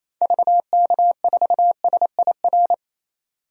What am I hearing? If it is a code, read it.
Morse code, heard as a keyed tone, about 28 wpm, 700 Hz. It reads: VK4SIR
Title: DXCC-100 Call Signs - 6 Characters - 28wpm